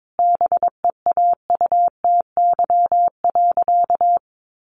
Morse code, heard as a keyed tone, about 22 wpm, 700 Hz. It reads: BEAUTY.
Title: Sentences from Top 1300 Words 22wpm